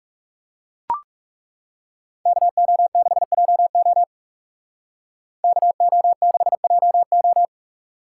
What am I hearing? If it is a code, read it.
Morse code, heard as a keyed tone, about 39 wpm, 700 Hz. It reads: KO6JO KO6JO